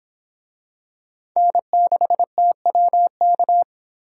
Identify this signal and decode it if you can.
Morse code, heard as a keyed tone, about 26 wpm, 700 Hz. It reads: N6TWK